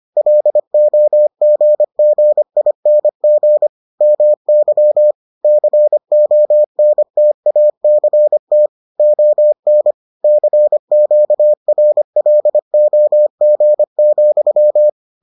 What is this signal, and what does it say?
Morse code, heard as a keyed tone, about 25 wpm, 600 Hz. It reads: LOGGING MY CONTACT ON CQRLOG,